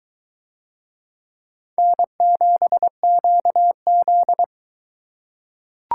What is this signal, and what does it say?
Morse code, heard as a keyed tone, about 21 wpm, 700 Hz. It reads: N7QZ